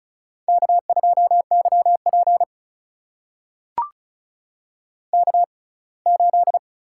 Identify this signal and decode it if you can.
Morse code, heard as a keyed tone, about 36 wpm, spaced as for 24 wpm, 700 Hz. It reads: K2YP K8
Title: USA Call Signs - 4 Characters - 6x Character Spacing - 35wpm